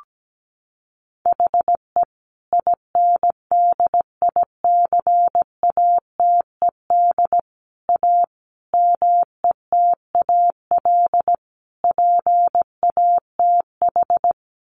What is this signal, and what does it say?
Morse code, heard as a keyed tone, about 17 wpm, 700 Hz. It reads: HE INDICATED A METAL PATH
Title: A Sound Of Thunder 17wpm